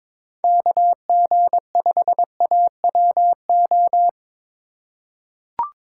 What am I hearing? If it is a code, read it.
Morse code, heard as a keyed tone, about 21 wpm, 700 Hz. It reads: KG5AWO
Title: USA Call Signs - 6 Characters - 4x Character Spacing - 22wpm